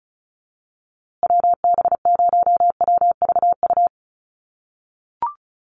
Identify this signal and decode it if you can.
Morse code, heard as a keyed tone, about 34 wpm, 700 Hz. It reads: WB0WVU
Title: USA Call Signs Repeated 2x 35wpm